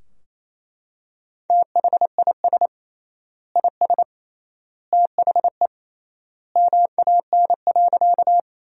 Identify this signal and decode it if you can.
Morse code, heard as a keyed tone, about 27 wpm, 700 Hz. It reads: THIS IS THE MAN.
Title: 3x Word-Spacing - Sentences from Top 200 Words - 28wpm